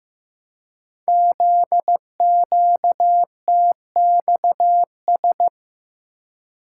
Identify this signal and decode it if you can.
Morse code, heard as a keyed tone, about 15 wpm, 700 Hz. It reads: ZQTXS
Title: Code Groups - 5 Letters - 15wpm